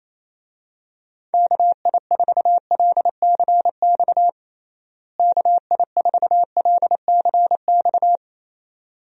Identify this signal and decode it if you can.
Morse code, heard as a keyed tone, about 27 wpm, 700 Hz. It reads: KI4LCX KI4LCX
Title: USA Call Signs - 6 Characters - Repeated 2x - 28wpm